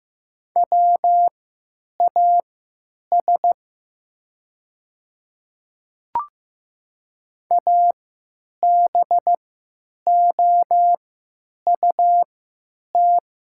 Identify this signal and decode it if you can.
Morse code, heard as a keyed tone, about 15 wpm, spaced as for 9 wpm, 700 Hz. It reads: WAS ABOUT
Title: 3x Character Spacing - Top 200 Words - 15wpm